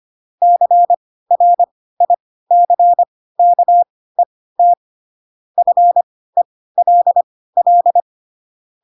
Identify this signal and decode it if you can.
Morse code, heard as a keyed tone, about 25 wpm, spaced as for 16 wpm, 700 Hz. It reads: CRICKET FELL
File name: Pinocchio / Pinocchio ch4 10 25 ik1hge